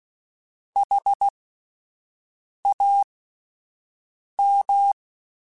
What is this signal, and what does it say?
Morse code, heard as a keyed tone, about 16 wpm, spaced as for 5 wpm, 800 Hz. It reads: HAM
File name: Alphabetical / Alphabetical mp3s / Hs @5